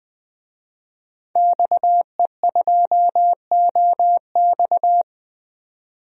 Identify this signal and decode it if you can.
Morse code, heard as a keyed tone, about 20 wpm, 700 Hz. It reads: XE2OX
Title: DXCC-100 Call Signs - 5 Characters - 20wpm